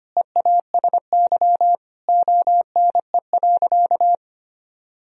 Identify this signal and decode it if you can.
Morse code, heard as a keyed tone, about 25 wpm, 700 Hz. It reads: EASY ONE.
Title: Sentences from Top 1000 Words - Review Part 2 - 25wpm